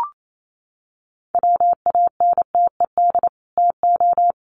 Morse code, heard as a keyed tone, about 28 wpm, 700 Hz. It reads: WANTED TO
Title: Sets of 2 Words 28wpm